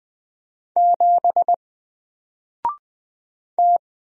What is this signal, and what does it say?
Morse code, heard as a keyed tone, about 20 wpm, 700 Hz. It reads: 7 T
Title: Single Letter-Number 20wpm